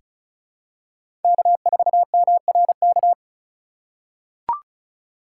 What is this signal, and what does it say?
Morse code, heard as a keyed tone, about 36 wpm, 700 Hz. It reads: K4MRK
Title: USA Call Signs - 5 Characters - 2x Character Spacing - 35wpm